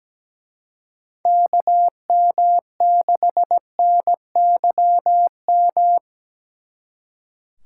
Morse code, heard as a keyed tone, about 20 wpm, 700 Hz. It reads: KM6NYM
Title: Speed Racing - USA 2x3 Call Signs 17wpm